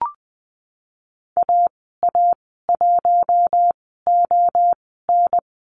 Morse code, heard as a keyed tone, about 20 wpm, spaced as for 15 wpm, 700 Hz. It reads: AA1ON
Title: 048 Lesson 9g - Number 4 - Call Signs 20wpm